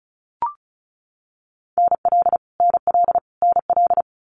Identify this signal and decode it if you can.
Morse code, heard as a keyed tone, about 36 wpm, 700 Hz. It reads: NL NL NL
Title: Canadian Provinces and Territories - Repeated 3x - 35wpm